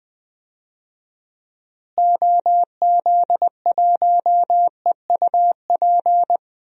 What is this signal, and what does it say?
Morse code, heard as a keyed tone, about 20 wpm, 700 Hz. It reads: OZ1EUP
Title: DXCC-100 Call Signs - 6 Characters - Repeated 2x - 20wpm